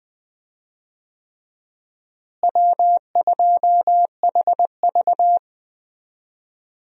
Morse code, heard as a keyed tone, about 20 wpm, 700 Hz. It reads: W2HV
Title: USA Call Signs - 4 Characters - Repeated 3x - 20wpm